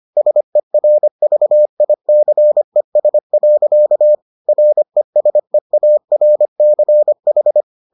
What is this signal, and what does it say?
Morse code, heard as a keyed tone, about 25 wpm, 600 Hz. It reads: SERVICES. RESEARCH